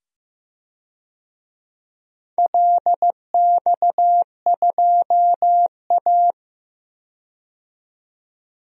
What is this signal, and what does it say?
Morse code, heard as a keyed tone, about 15 wpm, 700 Hz. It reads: LX2A
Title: DXCC-100 Call Signs - 4 Characters - 15wpm